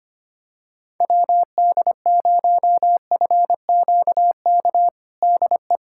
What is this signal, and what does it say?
Morse code, heard as a keyed tone, about 25 wpm, 700 Hz. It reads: WD0FQK DE
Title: USA Callsign DE Callsign 25wpm